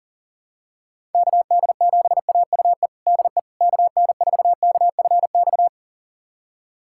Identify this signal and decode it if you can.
Morse code, heard as a keyed tone, about 37 wpm, 700 Hz. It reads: KD7AUE DE KN4KFX
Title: USA Callsign DE Callsign - 6 Characters - 40wpm